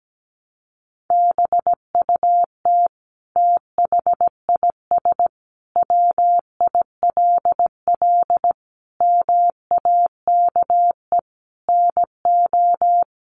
Morse code, heard as a keyed tone, about 17 wpm, 700 Hz. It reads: BUT THIS WILL MAKE NO
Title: Sentences from Top 800 Words 17wpm